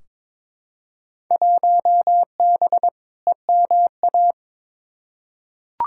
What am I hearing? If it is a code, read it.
Morse code, heard as a keyed tone, about 22 wpm, 700 Hz. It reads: 1B EMA
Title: Field Day Exchanges - 22wpm